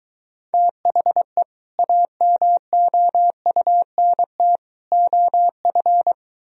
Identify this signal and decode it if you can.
Morse code, heard as a keyed tone, about 22 wpm, 700 Hz. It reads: THE AMOUNT OF